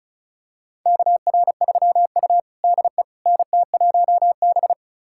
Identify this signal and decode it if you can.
Morse code, heard as a keyed tone, about 36 wpm, 700 Hz. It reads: KR3U DE NT1B
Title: USA Callsign DE Callsign - 4 Characters - 35wpm